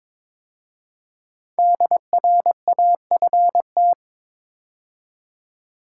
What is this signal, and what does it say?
Morse code, heard as a keyed tone, about 22 wpm, 700 Hz. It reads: DRAFT
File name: 5 Letter Words - First 3 Letters Given - 22wpm